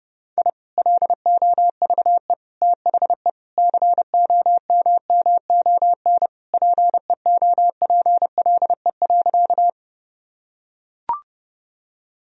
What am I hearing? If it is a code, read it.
Morse code, heard as a keyed tone, about 30 wpm, 700 Hz. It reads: I LOVE THE COMMON PEOPLE.
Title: Sentences from Top 1000 Words - Review Part 3 - 30wpm